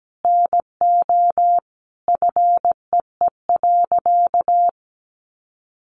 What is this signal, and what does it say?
Morse code, heard as a keyed tone, about 17 wpm, 700 Hz. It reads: NO FEE.
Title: Sentences from Top 1500 Words 17wpm